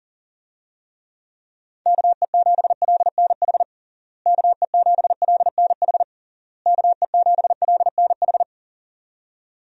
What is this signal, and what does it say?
Morse code, heard as a keyed tone, about 40 wpm, 700 Hz. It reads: KE7LNH KE7LNH KE7LNH
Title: USA Call Signs - 6 Characters - Repeated 3x - 40wpm